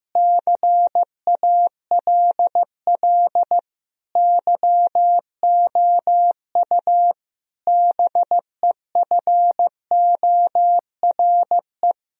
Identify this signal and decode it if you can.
Morse code, heard as a keyed tone, about 15 wpm, 700 Hz. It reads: CALL YOU BEFORE
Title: Sentences from Top 1000 Words - Review Part 2 - 15wpm